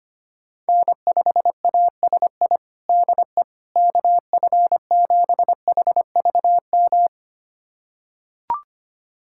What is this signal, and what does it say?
Morse code, heard as a keyed tone, about 25 wpm, 700 Hz. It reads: N5ASI DE KF7HVM
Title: USA Callsign DE Callsign 25wpm